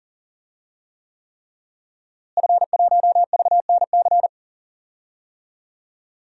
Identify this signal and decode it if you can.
Morse code, heard as a keyed tone, about 40 wpm, 700 Hz. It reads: F1VNC
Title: DXCC-100 Call Signs - 5 Characters - Repeated 3x - 40wpm